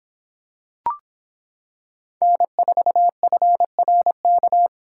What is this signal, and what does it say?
Morse code, heard as a keyed tone, about 26 wpm, 700 Hz. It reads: N4FRK